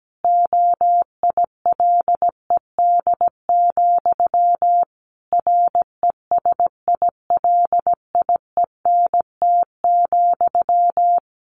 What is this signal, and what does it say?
Morse code, heard as a keyed tone, about 17 wpm, 700 Hz. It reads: OILED, RESILIENT,
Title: A Sound Of Thunder 17wpm